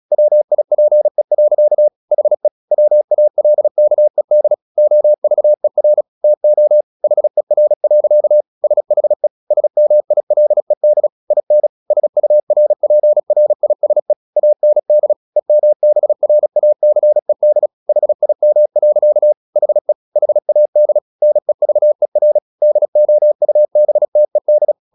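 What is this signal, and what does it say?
Morse code, heard as a keyed tone, about 37 wpm, 600 Hz. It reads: WIPE. HE WALKED OVER TO HER. SHE SMILED IN SURPRISE AND EMBRACED HIM. HE HAD NEVER DOUBTED
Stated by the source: Morse practice at 36 wpm, text of Ken Follett's novel Never, from CW med Gnister.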